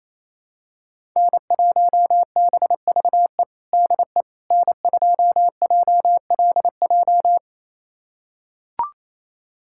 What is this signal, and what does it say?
Morse code, heard as a keyed tone, about 28 wpm, 700 Hz. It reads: N1BVE DE N2JLJ
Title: USA Callsign DE Callsign - 5 Characters - 28wpm